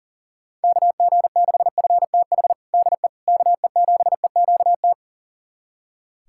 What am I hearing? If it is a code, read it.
Morse code, heard as a keyed tone, about 40 wpm, 690 Hz. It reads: KG6FTH DE KE7EQT
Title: USA Callsign DE Callsign - 6 Characters - 40wpm